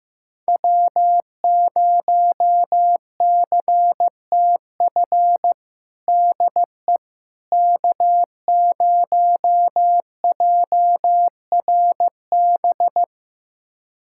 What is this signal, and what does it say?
Morse code, heard as a keyed tone, about 15 wpm, 700 Hz. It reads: W0CTF DE K0JRB
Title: USA Callsign DE Callsign - 5 Characters - 15wpm